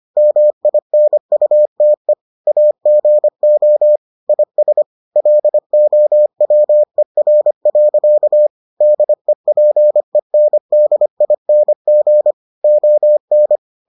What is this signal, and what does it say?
Morse code, heard as a keyed tone, about 25 wpm, 600 Hz. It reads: MINUTE AGO IS LOWER. DEPENDING ON